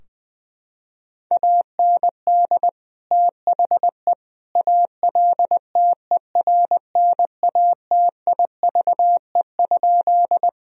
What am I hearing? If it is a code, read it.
Morse code, heard as a keyed tone, about 20 wpm, 700 Hz. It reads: AND THE ALTERNATIVE?
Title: Sentences from Top 1000 Words - Review Part 4 - 20wpm